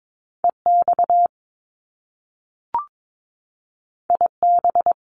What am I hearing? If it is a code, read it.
Morse code, heard as a keyed tone, about 22 wpm, 700 Hz. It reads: EX IB
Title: Top Two Letter Combinations in Words 22wpm